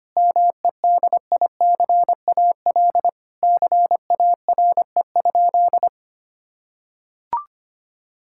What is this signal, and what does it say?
Morse code, heard as a keyed tone, about 25 wpm, 700 Hz. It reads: MEDICAL CARE?
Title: Sentences from Top 1000 Words - Review Part 5 - 25wpm